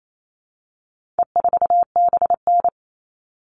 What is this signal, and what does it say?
Morse code, heard as a keyed tone, about 28 wpm, 700 Hz. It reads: E4BN
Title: All Cut Numbers - 4-Digit 28wpm